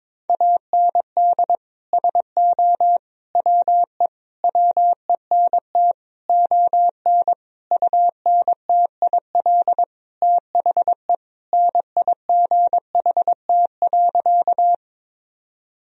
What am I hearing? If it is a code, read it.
Morse code, heard as a keyed tone, about 22 wpm, 700 Hz. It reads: AND SO WE WENT ON UNTIL THE NIGHT.
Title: Sentences from Top 1000 Words - Review Part 1 - 22wpm